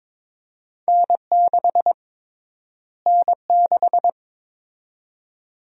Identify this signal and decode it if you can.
Morse code, heard as a keyed tone, about 22 wpm, 700 Hz. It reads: N6 N6
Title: USA Call Sign Prefixes - Repeated 2x - 22wpm